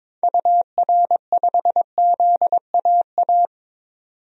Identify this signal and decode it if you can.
Morse code, heard as a keyed tone, about 22 wpm, 700 Hz. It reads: UR5ZAA
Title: DXCC-100 Call Signs - 6 Characters - 22wpm